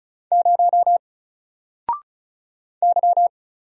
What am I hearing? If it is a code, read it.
Morse code, heard as a keyed tone, about 35 wpm, 700 Hz. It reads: 0 Y
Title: Single Character (Letter:Number:Punctuation:Prosign) Alt 35wpm